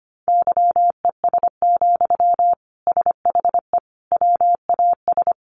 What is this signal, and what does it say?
Morse code, heard as a keyed tone, about 25 wpm, 700 Hz. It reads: YES, SHE WAS
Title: Sentences from Top 500 Words 25wpm